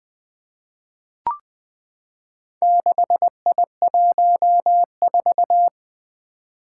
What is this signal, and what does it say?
Morse code, heard as a keyed tone, about 20 wpm, 700 Hz. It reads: 6I14
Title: Four Challenging Letter-Number Combinations 20wpm